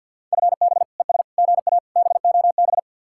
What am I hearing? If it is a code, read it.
Morse code, heard as a keyed tone, about 58 wpm, 700 Hz. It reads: FB ES GA BOB
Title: Speed-Racing Common QSO Phrases Encore - 50wpm